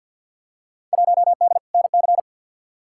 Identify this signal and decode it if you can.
Morse code, heard as a keyed tone, about 51 wpm, 700 Hz. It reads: 1D NC
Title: Field Day Exchanges - 50wpm